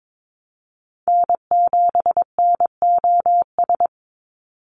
Morse code, heard as a keyed tone, about 22 wpm, 700 Hz. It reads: N7NOS
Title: USA Call Signs 22wpm